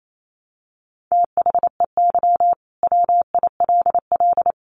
Morse code, heard as a keyed tone, about 28 wpm, 700 Hz. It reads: THEY WILL
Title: Sentences from Top 1500 Words 28wpm